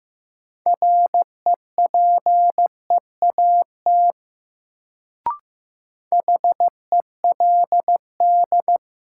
Morse code, heard as a keyed tone, about 15 wpm, 700 Hz. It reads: REPEAT HELD
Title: Sentences from Top 1600 Words - New Word Drill - 15wpm